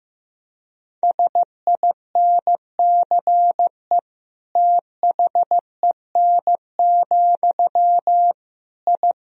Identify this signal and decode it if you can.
Morse code, heard as a keyed tone, about 15 wpm, 700 Hz. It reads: SINCE THEN, I
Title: Sentences from Top 1000 Words - Review Part 1 - 15wpm